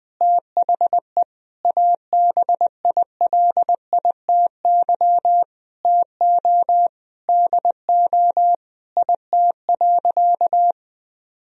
Morse code, heard as a keyed tone, about 19 wpm, 700 Hz. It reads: THE ABILITY TO DO IT.
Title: Sentences from Top 1000 Words - Review Part 3 - 20wpm